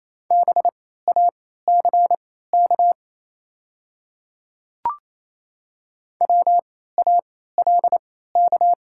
Morse code, heard as a keyed tone, about 27 wpm, spaced as for 16 wpm, 700 Hz. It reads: BACK WALK
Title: 3x Character Spacing - Top 300 Words - 28wpm